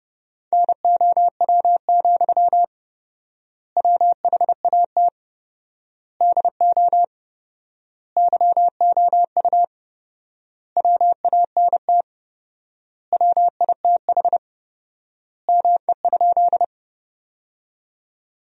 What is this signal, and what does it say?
Morse code, heard as a keyed tone, about 32 wpm, 700 Hz. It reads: NOW, WHAT DO YOU WANT WITH ME?
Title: 4x Word-Spacing - Sentences from Top 200 Words - 30wpm